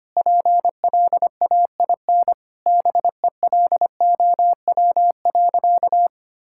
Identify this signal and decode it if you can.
Morse code, heard as a keyed tone, about 25 wpm, 700 Hz. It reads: PLAIN BELOW.
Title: Sentences from Top 1000 Words - Review Part 5 - 25wpm